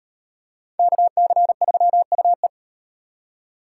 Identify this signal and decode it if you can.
Morse code, heard as a keyed tone, about 35 wpm, 700 Hz. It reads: KC3UE